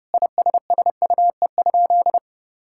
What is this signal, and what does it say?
Morse code, heard as a keyed tone, about 30 wpm, 700 Hz. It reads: ISSUE?
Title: Sentences from Top 1000 Words - Review Part 6 - 30wpm